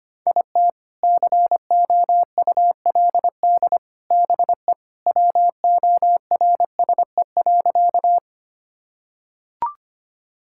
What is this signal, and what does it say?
Morse code, heard as a keyed tone, about 25 wpm, 700 Hz. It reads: IT COULD BE WORSE.
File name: practice / 25wpm / Sentences with 3-Word Phrases - 25wpm